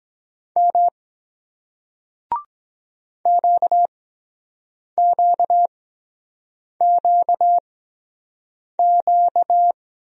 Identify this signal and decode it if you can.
Morse code, heard as a keyed tone, about 23 wpm, spaced as for 7 wpm, 700 Hz. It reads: M QQQQ